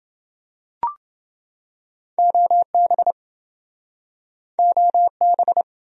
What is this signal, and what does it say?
Morse code, heard as a keyed tone, about 28 wpm, 700 Hz. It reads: OB OB